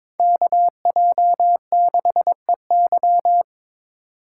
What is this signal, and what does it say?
Morse code, heard as a keyed tone, about 26 wpm, 700 Hz. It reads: KJ6EY
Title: Speed Racing - USA Call Signs 22wpm